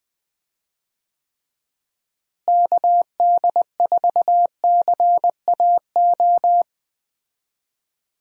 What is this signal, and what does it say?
Morse code, heard as a keyed tone, about 20 wpm, 700 Hz. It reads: KD4CAO
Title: USA Call Signs - 6 Characters - Repeated 2x - 20wpm